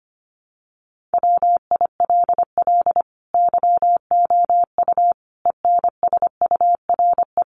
Morse code, heard as a keyed tone, about 25 wpm, 700 Hz. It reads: WILL YOU ENSURE
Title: Sentences from Top 1100 Words 25wpm